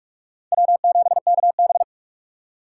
Morse code, heard as a keyed tone, about 46 wpm, 700 Hz. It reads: W7KB
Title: USA Call Signs - 4 Characters - 6x Character Spacing - 45wpm